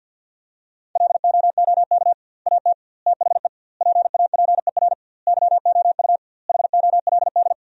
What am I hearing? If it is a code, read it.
Morse code, heard as a keyed tone, about 50 wpm, 700 Hz. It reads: LOOK AT THE PAPER YOU HOLD
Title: 2x Word-Spacing - Sentences from Top 400 Words - 50wpm